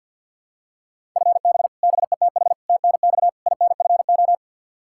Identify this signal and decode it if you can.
Morse code, heard as a keyed tone, about 51 wpm, 700 Hz. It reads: FB BETH TNX INFO
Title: Practice QSO - Exchange 3 - More Info - 50wpm